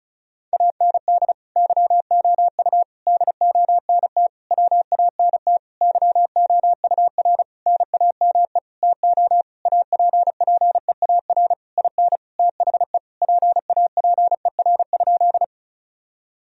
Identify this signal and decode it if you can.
Morse code, heard as a keyed tone, about 36 wpm, 700 Hz. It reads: AND YOU DONT WANT YOUR NAME TO APPEAR IN THE PAPER?
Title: Sentences from Top 1000 Words - Review Part 2 - 35wpm